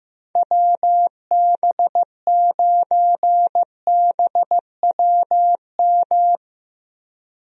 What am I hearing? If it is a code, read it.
Morse code, heard as a keyed tone, about 15 wpm, 700 Hz. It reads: WB9BWM
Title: USA Call Signs - 6 Characters - 15wpm